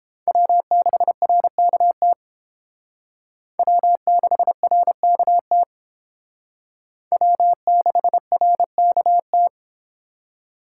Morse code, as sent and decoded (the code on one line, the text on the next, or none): .-- -.... .-. -.- - / .-- -.... .-. -.- - / .-- -.... .-. -.- -
W6RKT W6RKT W6RKT